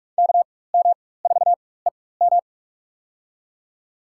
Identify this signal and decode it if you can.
Morse code, heard as a keyed tone, about 42 wpm, spaced as for 21 wpm, 690 Hz. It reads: KM4EM
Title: USA Call Signs - 5 Characters - 4x Character Spacing - 45wpm